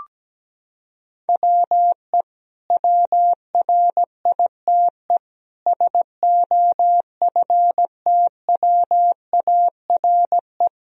Morse code, heard as a keyed tone, about 17 wpm, 700 Hz. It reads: WE WRITE SOFTWARE
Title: Sentences from Top 1000 Words - Review Part 3 - 17wpm